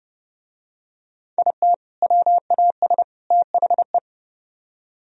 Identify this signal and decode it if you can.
Morse code, heard as a keyed tone, about 31 wpm, 700 Hz. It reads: IT WAS THE